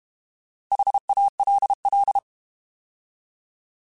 Morse code, heard as a keyed tone, about 32 wpm, 800 Hz. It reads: HALL
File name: Alphabetical / Alphabetical mp3s / Hs @30